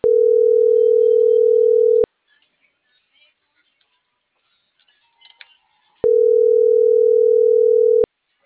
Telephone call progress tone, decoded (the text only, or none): ringback tone